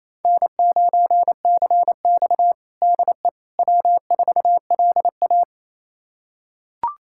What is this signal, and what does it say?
Morse code, heard as a keyed tone, about 28 wpm, 700 Hz. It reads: N9CX DE W4LA
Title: USA Callsign DE Callsign - 4 Characters - 28wpm